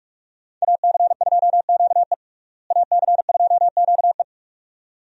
Morse code, heard as a keyed tone, about 46 wpm, 700 Hz. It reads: AC2QE AC2QE
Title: USA Call Signs - 5 Characters - Repeated 2x - 45wpm